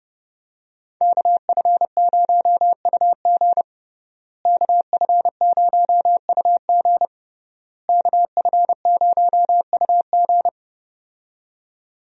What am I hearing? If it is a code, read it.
Morse code, heard as a keyed tone, about 30 wpm, 700 Hz. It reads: KF0UG KF0UG KF0UG
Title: USA Call Signs - 5 Characters - Repeated 3x - 30wpm